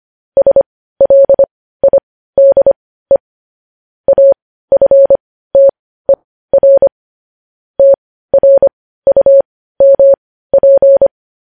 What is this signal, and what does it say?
Morse code, heard as a keyed tone, about 25 wpm, spaced as for 15 wpm, 550 Hz. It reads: SLIDE AFTER TRUMP
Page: news headlines in morse code at 15 wpm